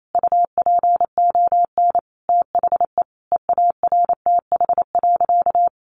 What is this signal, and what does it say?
Morse code, heard as a keyed tone, about 28 wpm, 700 Hz. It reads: UPON THE EARTH.
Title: Sentences from Top 1200 Words 28wpm